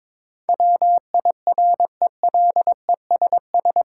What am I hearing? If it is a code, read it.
Morse code, heard as a keyed tone, about 22 wpm, 700 Hz. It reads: WIRELESS
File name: Difficult Words with Hints - 22wpm